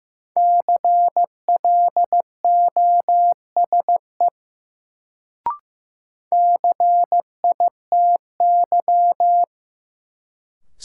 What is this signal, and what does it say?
Morse code, heard as a keyed tone, about 15 wpm, 700 Hz. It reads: CLOSE CITY
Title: Sentences from Top 300 Words - New Word Drill - 15wpm